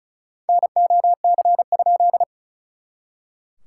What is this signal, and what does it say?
Morse code, heard as a keyed tone, about 34 wpm, 690 Hz. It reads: NOC?
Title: SOTA QSO Practice 35wpm